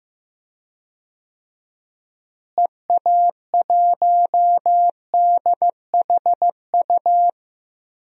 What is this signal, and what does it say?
Morse code, heard as a keyed tone, about 15 wpm, 700 Hz. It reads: EA1DHU
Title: DXCC-100 Call Signs - 6 Characters - Repeated 3x - 15wpm